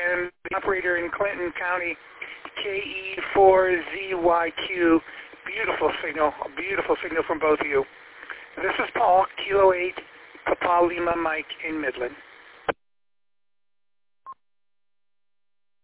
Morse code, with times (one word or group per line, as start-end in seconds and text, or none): none